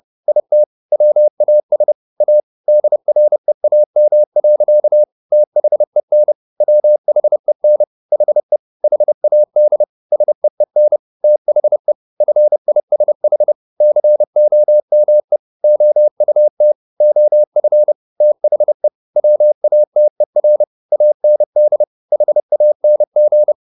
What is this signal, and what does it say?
Morse code, heard as a keyed tone, about 30 wpm, 600 Hz. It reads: IT WAS A DREAM. THEN WHEN HE HAD SEEN THE FISH COME OUT OF THE WATER AND HANG